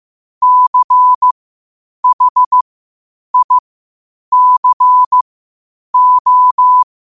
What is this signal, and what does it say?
Morse code, heard as a keyed tone, about 15 wpm, spaced as for 8 wpm, 1000 Hz. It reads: CHICO
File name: qotdfiles / 8wpm